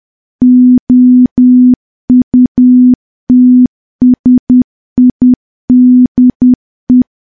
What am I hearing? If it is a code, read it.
Morse code, heard as a keyed tone, about 10 wpm, 250 Hz. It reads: OUTSIDE
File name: morse / morse-training-10WPM